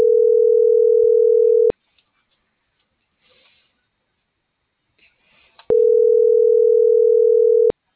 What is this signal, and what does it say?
Ringback tone (North America), 440 + 480 Hz.